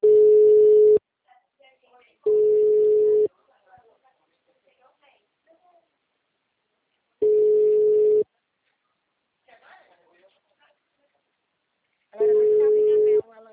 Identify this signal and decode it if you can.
Ringback tone (Europe, 425 Hz), with speech.